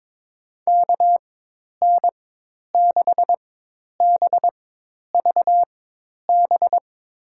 Morse code, heard as a keyed tone, about 21 wpm, spaced as for 10 wpm, 700 Hz. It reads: KN6BVB
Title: USA Call Signs - 4x Character Spacing - 22wpm